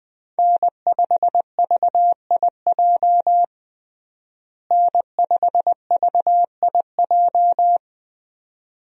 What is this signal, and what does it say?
Morse code, heard as a keyed tone, about 20 wpm, 700 Hz. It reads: N5VIJ N5VIJ
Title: USA Call Signs - 5 Characters - Repeated 2x - 20wpm